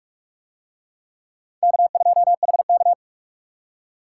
Morse code, heard as a keyed tone, about 39 wpm, 700 Hz. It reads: K2HK